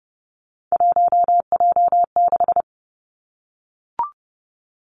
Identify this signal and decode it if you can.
Morse code, heard as a keyed tone, about 30 wpm, 700 Hz. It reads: 1J6
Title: Three Challenging Letter-Number Combinations 30wpm